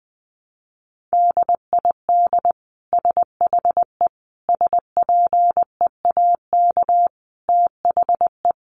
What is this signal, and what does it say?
Morse code, heard as a keyed tone, about 20 wpm, 700 Hz. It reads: DID SHE SPEAK THE
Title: Sentences from Top 1100 Words 20wpm